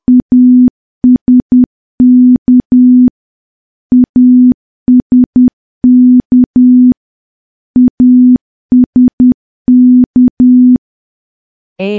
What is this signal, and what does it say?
Morse code, heard as a keyed tone, about 10 wpm, 250 Hz. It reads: ASK ASK ASK
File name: morse / morse-training-10WPM